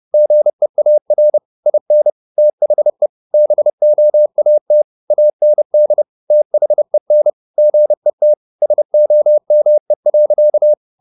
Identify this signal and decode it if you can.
Morse code, heard as a keyed tone, about 30 wpm, 600 Hz. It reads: GEAR IN THE BOAT AND THEN GET SOME.
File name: the old man and the sea / the old man and the sea 1 30wpm